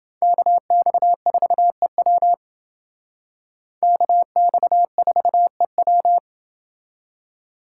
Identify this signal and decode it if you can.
Morse code, heard as a keyed tone, about 28 wpm, 700 Hz. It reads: KX4EW KX4EW